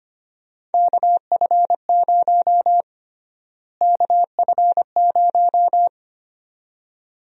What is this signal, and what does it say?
Morse code, heard as a keyed tone, about 25 wpm, 700 Hz. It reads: KF0 KF0
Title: USA Call Sign Prefixes - Repeated 2x - 25wpm